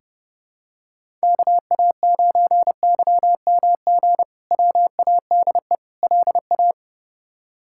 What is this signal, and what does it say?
Morse code, heard as a keyed tone, about 32 wpm, 700 Hz. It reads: KA9YMG WADE LA
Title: K1USN SST QSO Practice - 30wpm